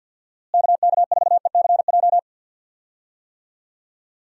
Morse code, heard as a keyed tone, about 46 wpm, 700 Hz. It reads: KK4ECJ
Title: USA Call Signs - 6 Characters - Repeated 3x - 50wpm